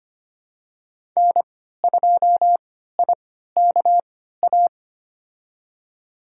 Morse code, heard as a keyed tone, about 25 wpm, spaced as for 14 wpm, 700 Hz. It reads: N2IKA
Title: USA Call Signs - 3x Character Spacing - 25wpm